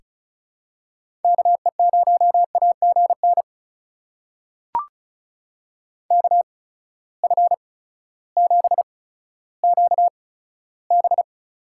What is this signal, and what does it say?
Morse code, heard as a keyed tone, about 36 wpm, spaced as for 16 wpm, 700 Hz. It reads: KE0AGN KF7QB
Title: USA Call Signs - 8x Character Spacing - 35wpm